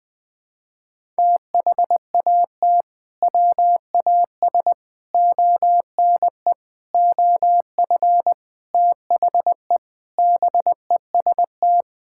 Morse code, heard as a keyed tone, about 20 wpm, 700 Hz. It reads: THAT WAS ONE OF THE BEST
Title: Sentences with 5-Word Phrases - 20wpm